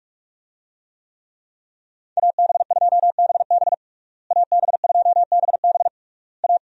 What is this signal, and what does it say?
Morse code, heard as a keyed tone, about 46 wpm, 700 Hz. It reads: AB2BB AB2BB A
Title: USA Call Signs - 5 Characters - Repeated 3x - 45wpm